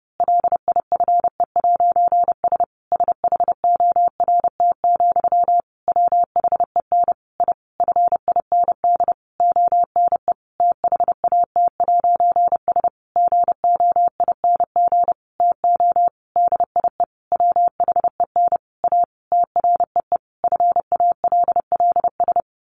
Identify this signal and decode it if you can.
Morse code, heard as a keyed tone, about 30 wpm, 700 Hz. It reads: LIFE'S SHORT, WHEN I FIND ONE THAT'S GOING TO DIE WHEN A TREE FALLS
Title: A Sound Of Thunder 30wpm